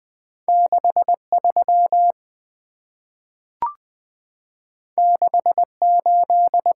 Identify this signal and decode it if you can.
Morse code, heard as a keyed tone, about 20 wpm, 700 Hz. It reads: 63 68
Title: Two Letter-Number Combinations - 20wpm